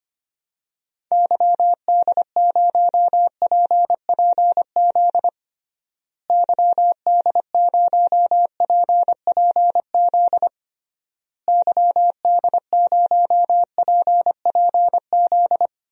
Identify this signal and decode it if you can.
Morse code, heard as a keyed tone, about 25 wpm, 700 Hz. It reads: YD0PPZ YD0PPZ YD0PPZ
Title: DXCC-100 Call Signs - 6 Characters - Repeated 3x - 25wpm